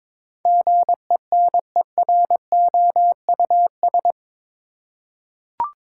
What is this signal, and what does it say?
Morse code, heard as a keyed tone, about 22 wpm, 700 Hz. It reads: GENEROUS
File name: practice / 22wpm / Eight Letter Words - 22wpm